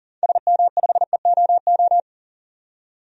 Morse code, heard as a keyed tone, about 40 wpm, 700 Hz. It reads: SM5EOO
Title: DXCC-100 Call Signs - 6 Characters - 40wpm